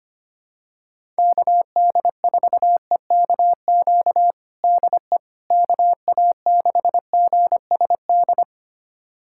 Morse code, heard as a keyed tone, about 25 wpm, 700 Hz. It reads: KD4EKQ DE KA6GSD
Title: USA Callsign DE Callsign - 6 Characters - 25wpm